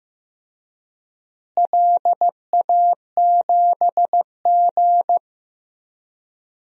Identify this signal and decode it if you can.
Morse code, heard as a keyed tone, about 15 wpm, 700 Hz. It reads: LA7G
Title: DXCC-100 Call Signs - 4 Characters - Repeated 3x - 15wpm